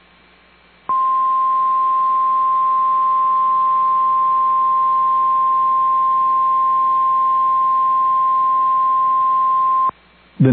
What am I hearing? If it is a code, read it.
Emergency Alert System (SAME) tones (the 1050 Hz attention tone). It is heard from 0.9 to 9.9 s.
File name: EAS Logs / KMNR